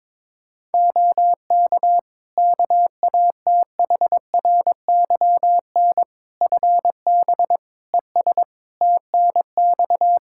Morse code, heard as a keyed tone, about 22 wpm, 700 Hz. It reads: OK KATHRYN FB ES TNX
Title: K1USN SST QSO Practice 22wpm